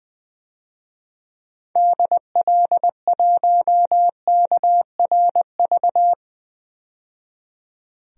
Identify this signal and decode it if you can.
Morse code, heard as a keyed tone, about 20 wpm, 700 Hz. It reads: DL1KRV
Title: DXCC-100 Call Signs - 6 Characters - 20wpm